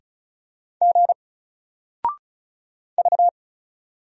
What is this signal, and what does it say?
Morse code, heard as a keyed tone, about 35 wpm, 700 Hz. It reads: G V
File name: Single-Letter-Number-35wpm